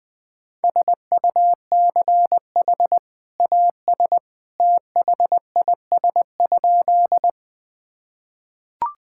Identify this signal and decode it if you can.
Morse code, heard as a keyed tone, about 20 wpm, 700 Hz. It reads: SUCH AS THIS?
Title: Sentences with 2-Word Phrases - 20wpm